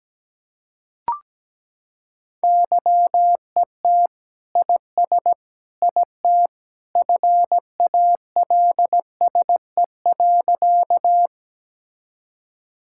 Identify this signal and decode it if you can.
Morse code, heard as a keyed tone, about 17 wpm, 700 Hz. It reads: YET IS IT FALSE.
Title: Sentences from Top 1600 Words 17wpm